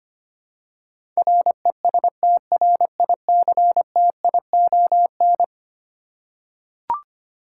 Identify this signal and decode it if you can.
Morse code, heard as a keyed tone, about 25 wpm, 700 Hz. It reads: RESTRICTION
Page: morse-code-ninja-practice-25wpm